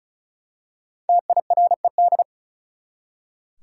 Morse code, heard as a keyed tone, about 36 wpm, 700 Hz. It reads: TIRED
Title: Five Letter Words - 35wpm